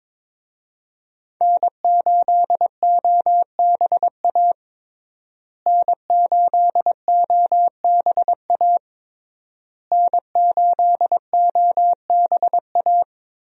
Morse code, heard as a keyed tone, about 22 wpm, 700 Hz. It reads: N8OBA N8OBA N8OBA
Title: USA Call Signs - 5 Characters - Repeated 3x - 22wpm